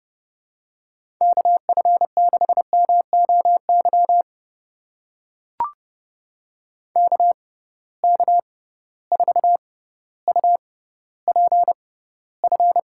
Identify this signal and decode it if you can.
Morse code, heard as a keyed tone, about 30 wpm, spaced as for 15 wpm, 700 Hz. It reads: KF6MOY KK4UPF
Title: USA Call Signs - 6 Characters - 6x Character Spacing - 30wpm